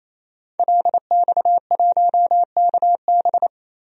Morse code, heard as a keyed tone, about 28 wpm, 700 Hz. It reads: LX1KB
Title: DXCC-100 Call Signs - 5 Characters - 28wpm